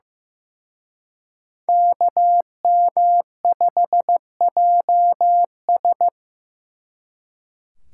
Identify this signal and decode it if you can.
Morse code, heard as a keyed tone, about 18 wpm, 700 Hz. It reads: KM5JS
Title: Speed Racing - USA Call Signs 15wpm